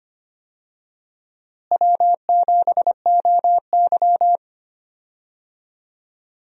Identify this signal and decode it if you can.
Morse code, heard as a keyed tone, about 25 wpm, 700 Hz. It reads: W7OY
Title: USA Call Signs - 4 Characters - Repeated 3x - 25wpm